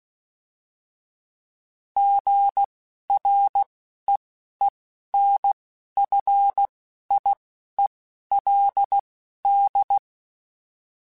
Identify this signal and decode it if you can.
Morse code, heard as a keyed tone, about 16 wpm, spaced as for 11 wpm, 800 Hz. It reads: GREENFIELD